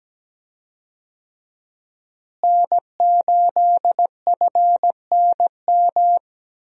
Morse code, heard as a keyed tone, about 17 wpm, 700 Hz. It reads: N8FNM
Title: USA Call Signs - 5 Characters - Repeated 3x - 17wpm